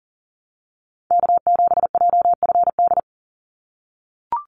Morse code, heard as a keyed tone, about 40 wpm, 700 Hz. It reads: K7JFD
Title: NA SOTA Chaser Callsigns 40wpm